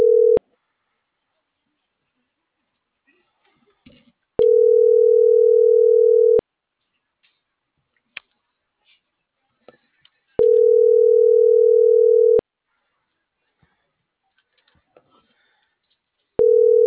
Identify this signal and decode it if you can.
Ringback tone (North America), 440 + 480 Hz.